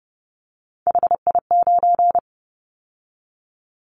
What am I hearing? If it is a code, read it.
Morse code, heard as a keyed tone, about 29 wpm, 700 Hz. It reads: HI9
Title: Callsign Prefixes for Top 100 DXCC - 30wpm